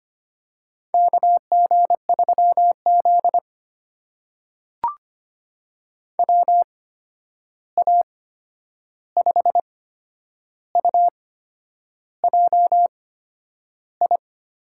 Morse code, heard as a keyed tone, about 25 wpm, spaced as for 9 wpm, 700 Hz. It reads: KG3Z WA5UJI
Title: USA Call Signs - 8x Character Spacing - 25wpm